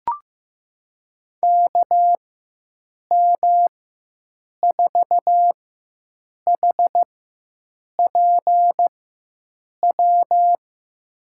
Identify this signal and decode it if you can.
Morse code, heard as a keyed tone, about 15 wpm, spaced as for 7 wpm, 700 Hz. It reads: KM4HPW
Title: USA Call Signs - 6 Characters - 4x Character Spacing - 15wpm